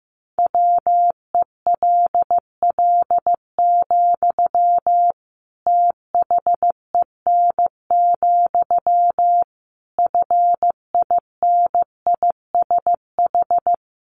Morse code, heard as a keyed tone, about 15 wpm, 700 Hz. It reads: WELL, THEN, FINISH
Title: Sentences from Top 1200 Words 15wpm